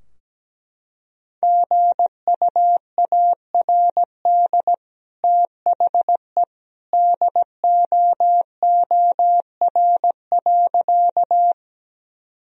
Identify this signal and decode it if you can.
Morse code, heard as a keyed tone, about 17 wpm, 700 Hz. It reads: GUARD THE DOOR.
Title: Sentences from Top 1000 Words - Review Part 4 - 17wpm